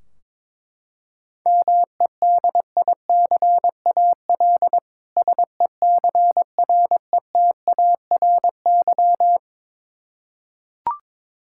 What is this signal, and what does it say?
Morse code, heard as a keyed tone, about 22 wpm, 700 Hz. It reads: MEDICAL SECRETARY
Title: Practice QSO - Exchange 3 - More Info - 22wpm